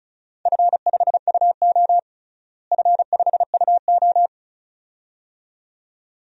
Morse code, heard as a keyed tone, about 34 wpm, 700 Hz. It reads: F5UO F5UO
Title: DXCC-100 Call Signs - 4 Characters - Repeated 2x - 35wpm